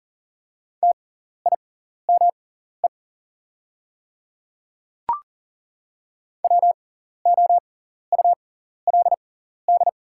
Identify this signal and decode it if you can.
Morse code, heard as a keyed tone, about 40 wpm, spaced as for 14 wpm, 690 Hz. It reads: TIME WOULD
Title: 6x Character Spacing - Top 100 Words - 40wpm